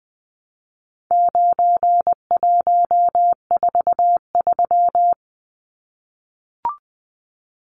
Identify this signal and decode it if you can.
Morse code, heard as a keyed tone, about 20 wpm, 700 Hz. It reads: 9143
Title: Numbers - 4-Digit 20wpm